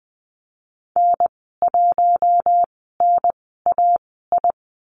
Morse code, heard as a keyed tone, about 20 wpm, spaced as for 15 wpm, 700 Hz. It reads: N1NAI
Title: 042 Lesson 8g - Number 1 - Call Signs 20wpm